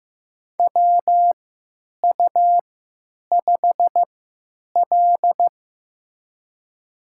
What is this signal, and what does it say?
Morse code, heard as a keyed tone, about 15 wpm, spaced as for 9 wpm, 700 Hz. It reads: WU5L
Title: USA Call Signs - 4 Characters - 3x Character Spacing - 15wpm